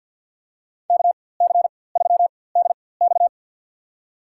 Morse code, heard as a keyed tone, about 47 wpm, spaced as for 23 wpm, 700 Hz. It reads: KC3DX
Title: USA Call Signs - 4x Character Spacing - 50wpm